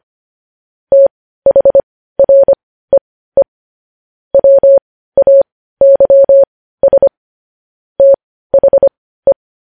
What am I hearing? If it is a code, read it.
Morse code, heard as a keyed tone, about 25 wpm, spaced as for 15 wpm, 550 Hz. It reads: THREE WAYS THE
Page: News Headlines in Morse Code at 15 WPM